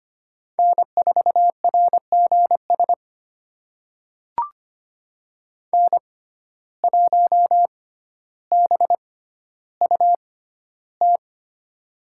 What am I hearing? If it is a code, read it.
Morse code, heard as a keyed tone, about 25 wpm, spaced as for 13 wpm, 700 Hz. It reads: N4RGS N1BUT
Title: USA Call Signs - 5 Characters - 6x Character Spacing - 25wpm